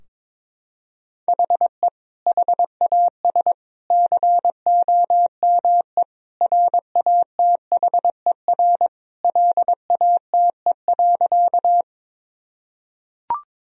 Morse code, heard as a keyed tone, about 22 wpm, 700 Hz. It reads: HE HAS COME RATHER LATE.
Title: Sentences from Top 1000 Words - Review Part 4 - 22wpm